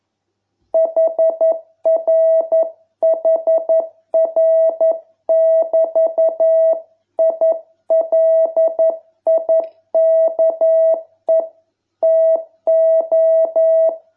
Morse code, heard as a keyed tone, about 11 wpm, 660 Hz. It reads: HRHR<BT>I LIKE TO